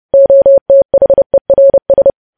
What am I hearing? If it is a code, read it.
Morse code, heard as a keyed tone, about 30 wpm, 550 Hz. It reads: OTHERS
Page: news headlines in morse code at 30 wpm